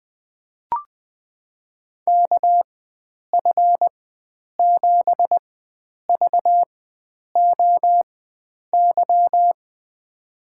Morse code, heard as a keyed tone, about 20 wpm, spaced as for 9 wpm, 700 Hz. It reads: KF7VOY
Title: USA Call Signs - 6 Characters - 4x Character Spacing - 20wpm